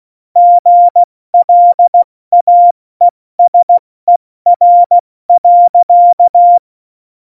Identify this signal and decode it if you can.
Morse code, heard as a keyed tone, about 16 wpm, spaced as for 14 wpm, 700 Hz. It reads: GLAESER.
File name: Morsekurs / Witze / Witz 17.Dezember 2019, 70 F80